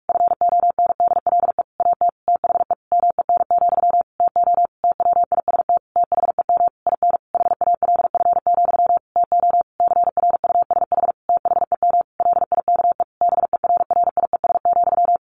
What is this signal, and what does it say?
Morse code, heard as a keyed tone, about 44 wpm, 700 Hz. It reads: FONDLE AT THE MEN, TO TWIST THEM IN HALF, TO CRUSH THEM LIKE BERRIES,
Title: A Sound Of Thunder 45wpm